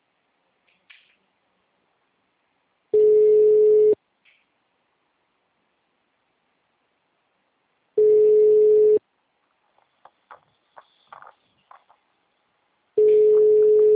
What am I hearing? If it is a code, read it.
Ringback tone (Europe, 425 Hz).
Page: Rio Centro - RJ